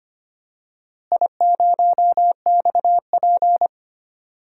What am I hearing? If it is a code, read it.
Morse code, heard as a keyed tone, about 25 wpm, 700 Hz. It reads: I0XP
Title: DXCC-100 Call Signs - 4 Characters - Repeated 3x - 25wpm